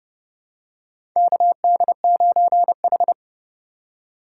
Morse code, heard as a keyed tone, about 30 wpm, 700 Hz. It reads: KD9H
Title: USA Call Signs - 4 Characters - 3x Character Spacing - 30wpm